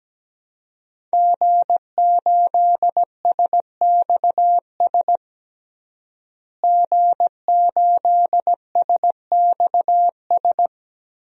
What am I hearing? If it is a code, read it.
Morse code, heard as a keyed tone, about 17 wpm, 700 Hz. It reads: G8SXS G8SXS
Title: DXCC-100 Call Signs - 5 Characters - Repeated 3x - 17wpm